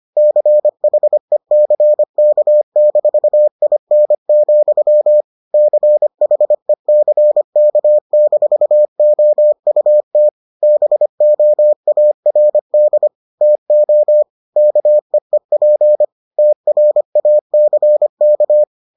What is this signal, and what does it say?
Morse code, heard as a keyed tone, about 25 wpm, 600 Hz. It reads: CHECK-IN, CHECK-OUT BOARD TO KEEP TRACK